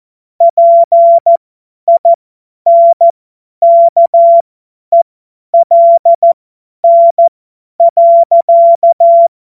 Morse code, heard as a keyed tone, about 14 wpm, spaced as for 10 wpm, 680 Hz. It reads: PINKELN.
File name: Morsekurs / Witze / Witz 12. Oktober 2021, 50 F70